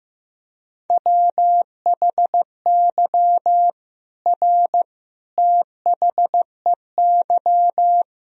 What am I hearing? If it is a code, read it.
Morse code, heard as a keyed tone, about 15 wpm, 700 Hz. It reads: WHY R THEY
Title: Sentences from Top 200 Words and Abbreviations 15wpm